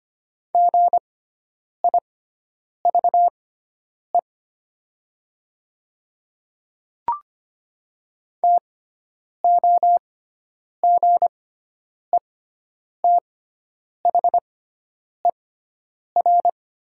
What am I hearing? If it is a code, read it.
Morse code, heard as a keyed tone, about 25 wpm, spaced as for 9 wpm, 700 Hz. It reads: GIVE TOGETHER
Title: 6x Character Spacing - Top 500 Words - 25wpm